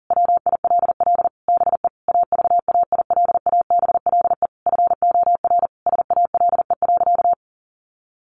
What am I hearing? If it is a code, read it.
Morse code, heard as a keyed tone, about 40 wpm, 700 Hz. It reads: WILL BE AVAILABLE FOR SALE.
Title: Sentences from Top 1400 Words 40wpm